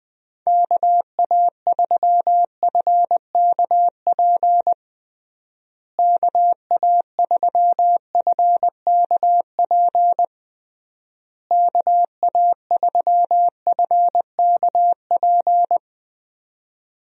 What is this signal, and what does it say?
Morse code, heard as a keyed tone, about 20 wpm, 700 Hz. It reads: KA3FKP KA3FKP KA3FKP
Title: USA Call Signs - 6 Characters - Repeated 3x - 20wpm